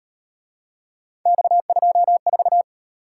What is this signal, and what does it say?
Morse code, heard as a keyed tone, about 34 wpm, 700 Hz. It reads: X24